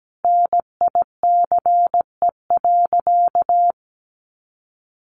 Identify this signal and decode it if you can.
Morse code, heard as a keyed tone, about 17 wpm, 700 Hz. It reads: NICE.
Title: Sentences from Top 1200 Words 17wpm